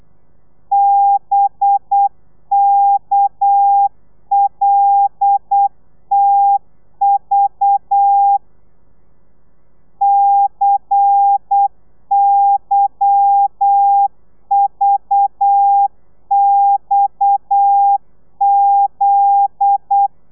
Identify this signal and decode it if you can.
Morse code, heard as a keyed tone, about 8 wpm, 800 Hz. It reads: BKLTV CYVXZ
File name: telegrafia / i2tfj / Lez10 35